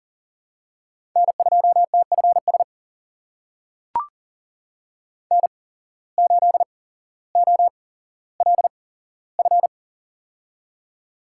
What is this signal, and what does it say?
Morse code, heard as a keyed tone, about 38 wpm, spaced as for 17 wpm, 700 Hz. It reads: N2TFS N8OLF
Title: USA Call Signs - 8x Character Spacing - 40wpm